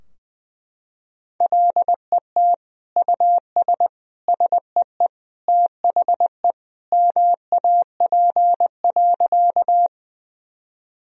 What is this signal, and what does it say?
Morse code, heard as a keyed tone, about 20 wpm, 700 Hz. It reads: LET US SEE THE MAP.
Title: Sentences from Top 1000 Words - Review Part 2 - 20wpm